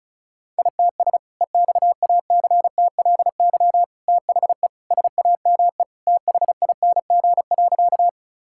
Morse code, heard as a keyed tone, about 34 wpm, 700 Hz. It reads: ITS EXACTLY THE SAME THING.
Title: Sentences from Top 1000 Words - Review Part 6 - 35wpm